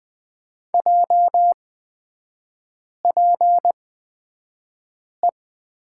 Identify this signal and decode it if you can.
Morse code, heard as a keyed tone, about 20 wpm, spaced as for 5 wpm, 700 Hz. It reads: JPE
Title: 224 Lesson 36e - BT - Three Characters 20wpm